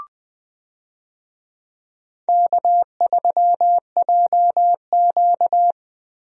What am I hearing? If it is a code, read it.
Morse code, heard as a keyed tone, about 20 wpm, 700 Hz. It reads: K3JQ
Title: USA Call Signs - 4 Characters - Repeated 3x - 20wpm